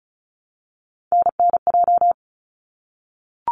Morse code, heard as a keyed tone, about 36 wpm, 700 Hz. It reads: NNJ
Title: ARRL Sections 35wpm